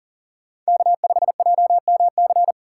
Morse code, heard as a keyed tone, about 40 wpm, 700 Hz. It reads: K5JMC
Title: USA Call Signs - 5 Characters - 4x Character Spacing - 40wpm